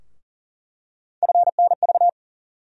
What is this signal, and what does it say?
Morse code, heard as a keyed tone, about 40 wpm, 700 Hz. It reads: FNV
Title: USA Call Sign Postfixes - 3 Letters 40wpm